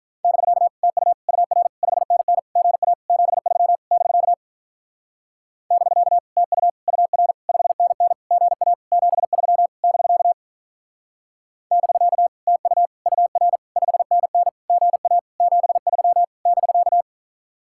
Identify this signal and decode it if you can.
Morse code, heard as a keyed tone, about 44 wpm, 700 Hz. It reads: <BK> TU UR 5NN GA 73 <BK> <BK> TU UR 5NN GA 73 <BK> <BK> TU UR 5NN GA 73 <BK>
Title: Speed-Racing Common QSO Phrases - 35wpm